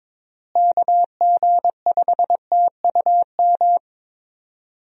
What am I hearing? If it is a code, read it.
Morse code, heard as a keyed tone, about 22 wpm, 700 Hz. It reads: KG5TUM
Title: USA Call Signs - 6 Characters - Repeated 3x - 22wpm